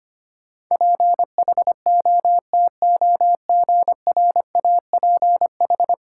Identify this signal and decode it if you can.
Morse code, heard as a keyed tone, about 25 wpm, 700 Hz. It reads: PHOTOGRAPH
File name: Ten Letter Words - 25wpm